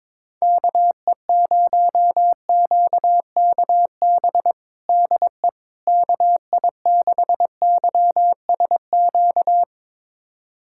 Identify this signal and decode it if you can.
Morse code, heard as a keyed tone, about 22 wpm, 700 Hz. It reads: KE0QKB DE KI6YSQ
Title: USA Callsign DE Callsign - 6 Characters - 22wpm